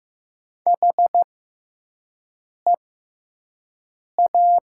Morse code, heard as a keyed tone, about 16 wpm, spaced as for 5 wpm, 700 Hz. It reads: HEA
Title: 6x Character Spacing - Top 500 Words - 15wpm